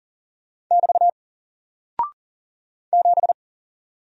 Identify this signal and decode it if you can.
Morse code, heard as a keyed tone, about 40 wpm, 700 Hz. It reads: <BT> 7
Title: Single Character (Letter:Number:Punctuation:Prosign) Alt 40wpm